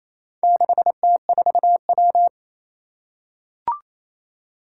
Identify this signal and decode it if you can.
Morse code, heard as a keyed tone, about 27 wpm, 700 Hz. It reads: 6T4W
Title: Four Challenging Letter-Number Combinations 28wpm